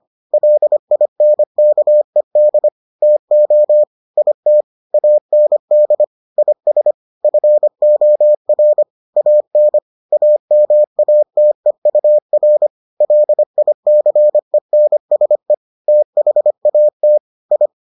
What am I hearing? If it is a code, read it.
Morse code, heard as a keyed tone, about 25 wpm, 600 Hz. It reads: LINKED TO IT AND IS FOR AN AMATEUR LICENSE THAT I